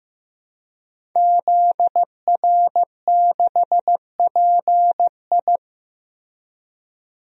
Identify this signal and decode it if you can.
Morse code, heard as a keyed tone, about 15 wpm, 700 Hz. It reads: ZR6PI
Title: DXCC-100 Call Signs - 5 Characters - Repeated 2x - 15wpm